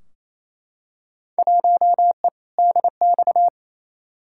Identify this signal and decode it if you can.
Morse code, heard as a keyed tone, about 28 wpm, 700 Hz. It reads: 1E DX
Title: Field Day Exchanges Encore - 28wpm